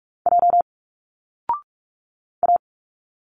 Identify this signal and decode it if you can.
Morse code, heard as a keyed tone, about 46 wpm, 700 Hz. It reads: J A
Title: Single Letters 45wpm